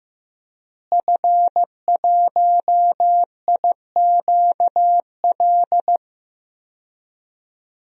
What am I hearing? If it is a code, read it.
Morse code, heard as a keyed tone, about 15 wpm, 700 Hz. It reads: F1IQL
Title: DXCC-100 Call Signs - 5 Characters - Repeated 2x - 15wpm